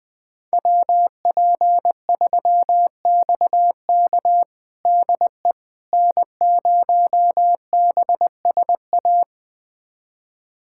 Morse code, heard as a keyed tone, about 20 wpm, 700 Hz. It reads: WP3XK DE N0BSA
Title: USA Callsign DE Callsign - 5 Characters - 20wpm